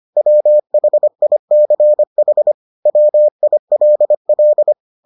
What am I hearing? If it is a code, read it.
Morse code, heard as a keyed tone, about 25 wpm, 600 Hz. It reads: WHICH WILL